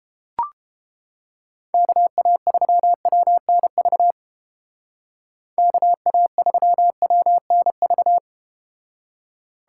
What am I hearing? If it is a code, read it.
Morse code, heard as a keyed tone, about 32 wpm, 700 Hz. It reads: KA3WNV KA3WNV